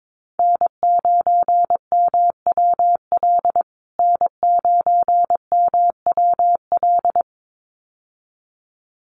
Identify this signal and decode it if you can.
Morse code, heard as a keyed tone, about 22 wpm, 700 Hz. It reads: N9MWL N9MWL
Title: USA Call Signs Repeated 2x 22wpm